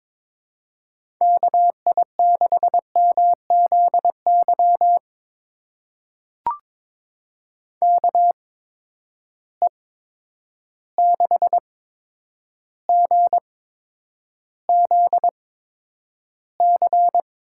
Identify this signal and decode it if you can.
Morse code, heard as a keyed tone, about 22 wpm, spaced as for 9 wpm, 700 Hz. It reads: KI6MZY KE6GZC
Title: USA Call Signs - 6 Characters - 8x Character Spacing - 22wpm